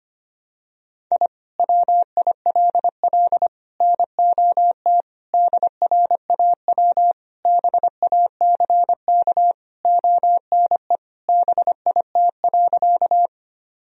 Morse code, heard as a keyed tone, about 25 wpm, 700 Hz. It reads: I WILL NOT DRAW BACK ONE BIT.
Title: Sentences from Top 1000 Words - Review Part 1 - 25wpm